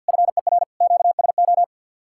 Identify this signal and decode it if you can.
Morse code, heard as a keyed tone, about 67 wpm, spaced as for 58 wpm, 700 Hz. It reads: FER QSO
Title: Speed-Racing Common QSO Phrases Encore - 50wpm